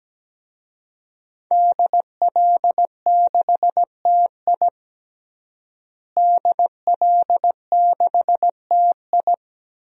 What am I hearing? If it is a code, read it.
Morse code, heard as a keyed tone, about 17 wpm, 700 Hz. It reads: DL6TI DL6TI
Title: DXCC-100 Call Signs - 5 Characters - Repeated 2x - 17wpm